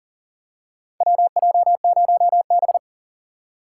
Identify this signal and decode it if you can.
Morse code, heard as a keyed tone, about 40 wpm, 700 Hz. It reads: WJ0B
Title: USA Call Signs - 4 Characters - 4x Character Spacing - 40wpm